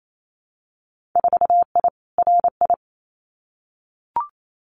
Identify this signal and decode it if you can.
Morse code, heard as a keyed tone, about 28 wpm, 700 Hz. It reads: 4I RI
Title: Winter Field Day Exchanges Encore 28wpm